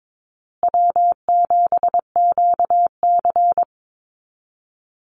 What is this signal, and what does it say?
Morse code, heard as a keyed tone, about 22 wpm, 700 Hz. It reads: W7QC
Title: Easy Call Signs Repeated 2x - 22wpm